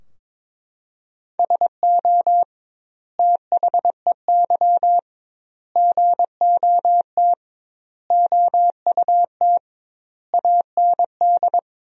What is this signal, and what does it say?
Morse code, heard as a keyed tone, about 22 wpm, 700 Hz. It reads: SO THEY GOT OUT AND
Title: 2x Word-Spacing - Sentences from Top 300 Words - 22wpm